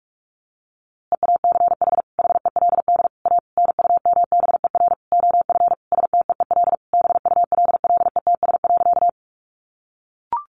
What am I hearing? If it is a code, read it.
Morse code, heard as a keyed tone, about 44 wpm, 700 Hz. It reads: EACH HELD A NUMBER OF STEEL BULLETS.
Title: A Sound Of Thunder 45wpm